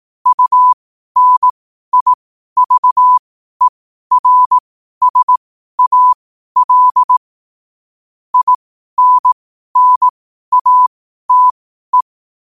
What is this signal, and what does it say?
Morse code, heard as a keyed tone, about 18 wpm, spaced as for 12 wpm, 1000 Hz. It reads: UNIVERSAL INNATE
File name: qotdfiles / 12wpm